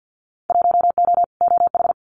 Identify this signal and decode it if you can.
Morse code, heard as a keyed tone, about 50 wpm, 700 Hz. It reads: 1O OH
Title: Winter Field Day Exchanges 50wpm